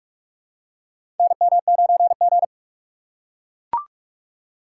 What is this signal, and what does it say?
Morse code, heard as a keyed tone, about 44 wpm, 700 Hz. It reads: NM9G
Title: USA Call Signs - 4 Characters - 4x Character Spacing - 45wpm